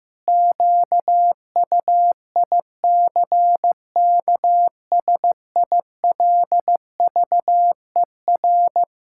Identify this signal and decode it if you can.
Morse code, heard as a keyed tone, about 15 wpm, 700 Hz. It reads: QUICKSILVER
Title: Pangram Challenge v2 - 15wpm